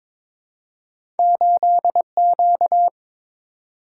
Morse code, heard as a keyed tone, about 22 wpm, 700 Hz. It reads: 8Q
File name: practice / 22wpm / Two Letter-Number Combinations - 22wpm